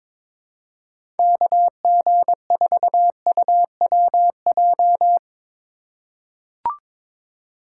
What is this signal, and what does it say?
Morse code, heard as a keyed tone, about 22 wpm, 700 Hz. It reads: KG4UWJ
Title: USA Call Signs - 6 Characters - 2x Character Spacing - 22wpm